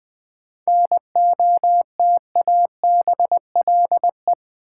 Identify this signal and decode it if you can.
Morse code, heard as a keyed tone, about 20 wpm, 700 Hz. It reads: NOTABLE
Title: 7 Letter Words - First 3 Letters Given - 20wpm